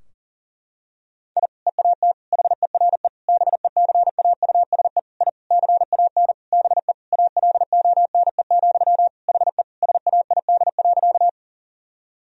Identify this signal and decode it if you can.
Morse code, heard as a keyed tone, about 40 wpm, 700 Hz. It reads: I EAT HERE BECAUSE I CAN BE ALONE, HE SAID.
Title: Sentences from Top 1000 Words - Review Part 3 - 40wpm